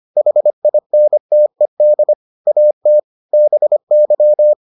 Morse code, heard as a keyed tone, about 25 wpm, 600 Hz. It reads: HINTED AT BY